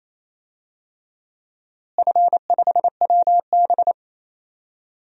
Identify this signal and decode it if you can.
Morse code, heard as a keyed tone, about 28 wpm, 700 Hz. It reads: F5WB
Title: DXCC-100 Call Signs - 4 Characters - Repeated 3x - 28wpm